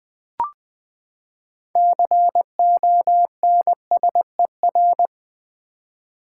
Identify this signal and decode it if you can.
Morse code, heard as a keyed tone, about 20 wpm, 700 Hz. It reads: CONSER
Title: Build A Word - 6 Letters At A Time - 12-Letter Words - 20wpm